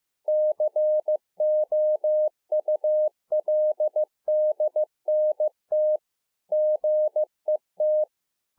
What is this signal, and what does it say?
Morse code, heard as a keyed tone, about 15 wpm, 610 Hz. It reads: COULDNT GET
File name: 2011-04-mp3 / A SMALL TRANCEIVER 2 15WPM